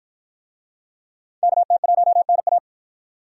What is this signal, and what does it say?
Morse code, heard as a keyed tone, about 53 wpm, 700 Hz. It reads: KT1NA